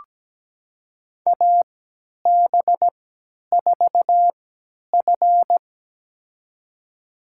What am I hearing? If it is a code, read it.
Morse code, heard as a keyed tone, about 17 wpm, spaced as for 10 wpm, 700 Hz. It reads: AB4F
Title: USA Call Signs - 4 Characters - 3x Character Spacing - 17wpm